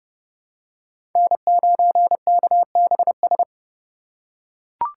Morse code, heard as a keyed tone, about 30 wpm, 700 Hz. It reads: N9KBS
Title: USA Call Signs - 2x Character Spacing - 30wpm